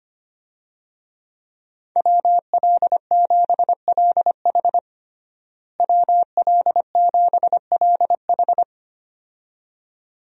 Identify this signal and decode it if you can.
Morse code, heard as a keyed tone, about 25 wpm, 700 Hz. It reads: WL7LH WL7LH
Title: USA Call Signs - 5 Characters - Repeated 2x - 25wpm